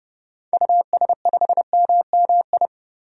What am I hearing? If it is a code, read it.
Morse code, heard as a keyed tone, about 30 wpm, 700 Hz. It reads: US5MMI
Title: DXCC-100 Call Signs - 6 Characters - 30wpm